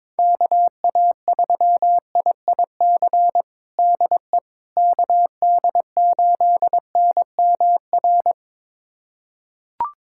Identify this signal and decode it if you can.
Morse code, heard as a keyed tone, about 22 wpm, 700 Hz. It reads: KA3IIC DE KD8NMR
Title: USA Callsign DE Callsign - 6 Characters - 22wpm